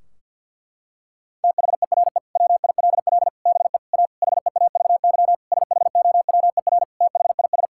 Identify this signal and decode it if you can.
Morse code, heard as a keyed tone, about 51 wpm, 700 Hz. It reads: THERE WILL BE A HEAVY SHOWER THIS
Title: Sentences from Top 1600 Words 50wpm